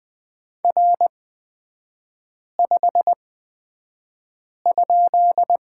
Morse code, heard as a keyed tone, about 20 wpm, spaced as for 5 wpm, 700 Hz. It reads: R5?
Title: 243 Lesson 39e - Letter z - Three Characters 20wpm